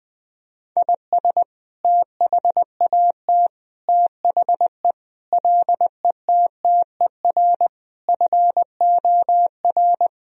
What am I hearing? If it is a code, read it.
Morse code, heard as a keyed tone, about 20 wpm, 700 Hz. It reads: IS THAT THE LETTER FOR
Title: Sentences from Top 1000 Words - Review Part 2 - 20wpm